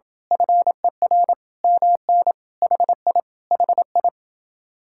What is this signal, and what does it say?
Morse code, heard as a keyed tone, about 27 wpm, 700 Hz. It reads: FER MN HI HI